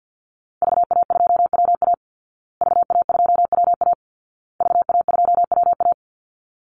Morse code, heard as a keyed tone, about 50 wpm, 700 Hz. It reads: VA2WA VA2WA VA2WA
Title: Call Signs Repeated 2x - 50wpm